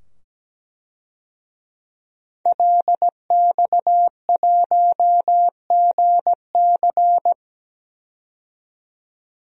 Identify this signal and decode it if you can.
Morse code, heard as a keyed tone, about 17 wpm, 700 Hz. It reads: LX1GC
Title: DXCC-100 Call Signs - 5 Characters - 17wpm